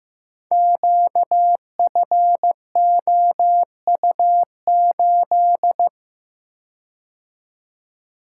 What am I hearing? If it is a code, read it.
Morse code, heard as a keyed tone, about 15 wpm, 700 Hz. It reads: QFOU8
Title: Code Groups - 5 Letter:Numbers - 15wpm